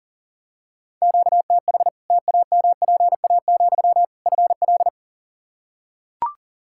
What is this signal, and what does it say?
Morse code, heard as a keyed tone, about 37 wpm, 700 Hz. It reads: QTH TAMPA, FL
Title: Practice QSO - Exchange 1 - Beginning - 40wpm